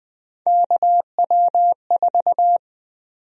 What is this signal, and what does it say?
Morse code, heard as a keyed tone, about 20 wpm, 700 Hz. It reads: KW4
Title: USA Call Sign Prefixes - 20wpm